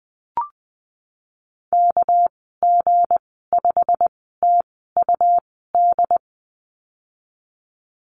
Morse code, heard as a keyed tone, about 20 wpm, spaced as for 15 wpm, 700 Hz. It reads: KG5TUD
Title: 207 Lesson 33g - Letter j - Call Signs 20wpm